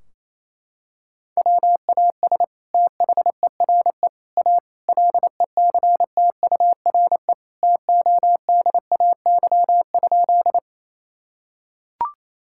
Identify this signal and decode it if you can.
Morse code, heard as a keyed tone, about 28 wpm, 700 Hz. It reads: WAS THERE A LECTURE TODAY?
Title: Sentences from Top 1600 Words 28wpm